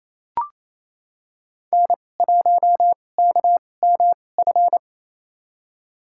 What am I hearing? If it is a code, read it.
Morse code, heard as a keyed tone, about 28 wpm, spaced as for 20 wpm, 700 Hz. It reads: N1KMF
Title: USA Call Signs - 5 Characters - 2x Character Spacing - 28wpm